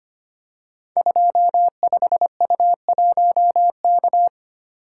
Morse code, heard as a keyed tone, about 25 wpm, 700 Hz. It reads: 25U1K
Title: Code Groups - 5 Letter:Numbers - 25wpm